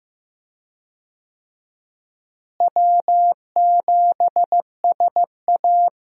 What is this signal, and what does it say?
Morse code, heard as a keyed tone, about 15 wpm, 700 Hz. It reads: W7SA
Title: USA Call Signs - 4 Characters - Repeated 2x - 15wpm